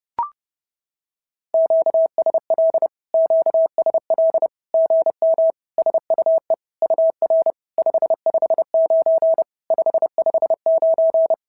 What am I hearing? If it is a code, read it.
Morse code, heard as a keyed tone, about 30 wpm, 650 Hz. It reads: QSL QSL GM SUE UR 559 559
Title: SOTA QSO Practice 30wpm